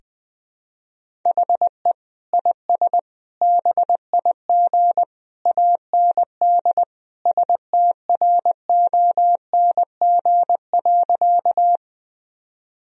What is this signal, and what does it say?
Morse code, heard as a keyed tone, about 20 wpm, 700 Hz. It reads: HE IS BIG AND STRONG.
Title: Sentences from Top 1000 Words - Review Part 3 - 20wpm